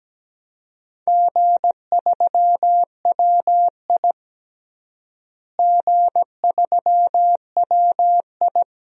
Morse code, heard as a keyed tone, about 17 wpm, 700 Hz. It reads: G3WI G3WI
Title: DXCC-100 Call Signs - 4 Characters - Repeated 2x - 17wpm